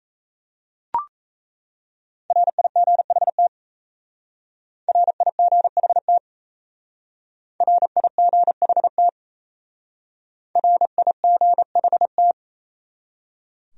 Morse code, heard as a keyed tone, about 36 wpm, 700 Hz. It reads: RIGHT RIGHT RIGHT RIGHT